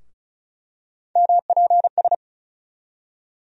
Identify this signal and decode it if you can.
Morse code, heard as a keyed tone, about 34 wpm, 700 Hz. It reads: MPS
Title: USA Call Sign Postfixes - 3 Letters 35wpm